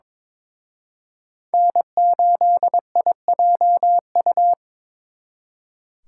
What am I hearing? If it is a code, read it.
Morse code, heard as a keyed tone, about 26 wpm, 700 Hz. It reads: N8IJU
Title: Speed Racing - USA 1x3 Call Signs 22wpm